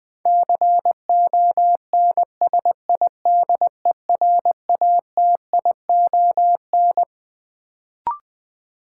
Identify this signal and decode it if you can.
Morse code, heard as a keyed tone, about 20 wpm, 700 Hz. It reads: CONSIDERATION
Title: Sentences from Top 1500 Words - New Word Drill - 20wpm